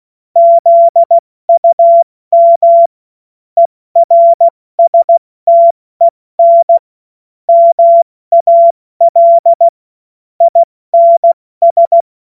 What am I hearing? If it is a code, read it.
Morse code, heard as a keyed tone, about 16 wpm, 680 Hz. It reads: ZUM ERSTEN MAL INS
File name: Morsekurs / Witze / Witz 13 Mai 2025 70 F80